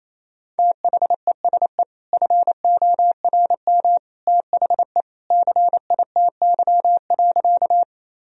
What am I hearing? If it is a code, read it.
Morse code, heard as a keyed tone, about 28 wpm, 700 Hz. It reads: THESE FORM THE CITY.
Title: Sentences from Top 1000 Words - Review Part 1 - 28wpm